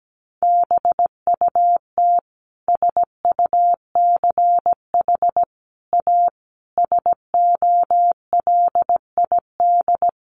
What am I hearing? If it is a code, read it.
Morse code, heard as a keyed tone, about 17 wpm, 700 Hz. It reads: BUT SUCH A SOLID
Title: Sentences from Top 1500 Words 17wpm